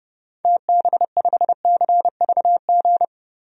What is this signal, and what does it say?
Morse code, heard as a keyed tone, about 31 wpm, 700 Hz. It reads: TB5CVG
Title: DXCC-100 Call Signs - 6 Characters - 30wpm